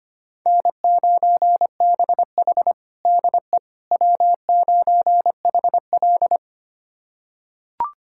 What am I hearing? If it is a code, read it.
Morse code, heard as a keyed tone, about 25 wpm, 700 Hz. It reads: N9BH DE W9HL
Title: USA Callsign DE Callsign - 4 Characters - 25wpm